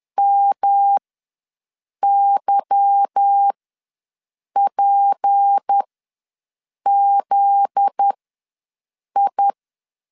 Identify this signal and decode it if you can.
Morse code, heard as a keyed tone, about 10 wpm, spaced as for 6 wpm, 800 Hz. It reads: MYPZI
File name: LEZIONI / 8 YJ45 / 8 YJ45 ws 10wpm 3843